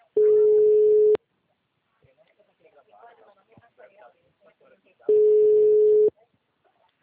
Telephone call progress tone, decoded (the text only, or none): ringback tone